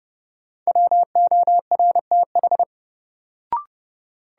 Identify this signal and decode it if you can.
Morse code, heard as a keyed tone, about 30 wpm, 700 Hz. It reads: WORTH
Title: Sentences from Top 800 Words - New Word Drill - 30wpm